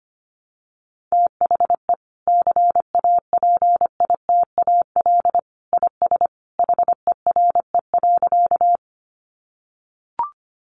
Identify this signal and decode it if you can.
Morse code, heard as a keyed tone, about 25 wpm, 700 Hz. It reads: THE CAPITAL IS HERE.
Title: Sentences from Top 1000 Words 25wpm